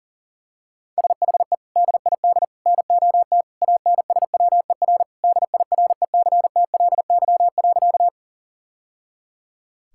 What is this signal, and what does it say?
Morse code, heard as a keyed tone, about 40 wpm, 700 Hz. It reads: SHE DID NOT ANSWER DIRECTLY.
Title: Sentences from Top 1000 Words - Review Part 5 - 40wpm